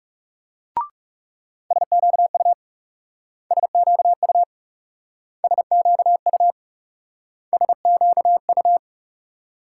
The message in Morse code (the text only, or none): SQU SQU SQU SQU